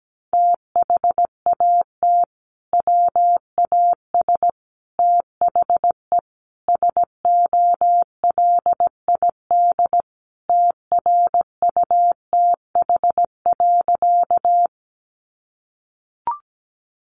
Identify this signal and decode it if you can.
Morse code, heard as a keyed tone, about 17 wpm, 700 Hz. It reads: THAT WAS THE SOLID TRUTH.
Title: Sentences from Top 1500 Words 17wpm